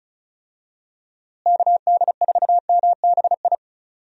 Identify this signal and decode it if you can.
Morse code, heard as a keyed tone, about 36 wpm, 700 Hz. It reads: KD4MBI
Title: USA Call Signs - 6 Characters - Repeated 2x - 35wpm